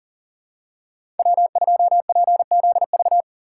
Morse code, heard as a keyed tone, about 40 wpm, 700 Hz. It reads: W2PZV
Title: USA Call Signs - 5 Characters - 2x Character Spacing - 40wpm